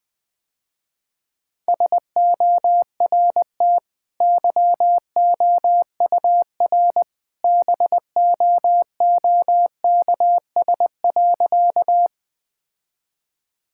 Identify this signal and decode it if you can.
Morse code, heard as a keyed tone, about 20 wpm, 700 Hz. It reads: SORT YOUR BOOKS.
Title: Easy Sentences with Hints - 20wpm